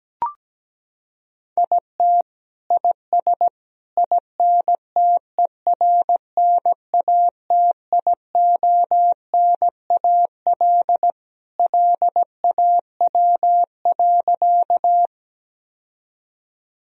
Morse code, heard as a keyed tone, about 17 wpm, 700 Hz. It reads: IT IS INTERNATIONAL LAW.
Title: Sentences from Top 1000 Words - Review Part 4 - 17wpm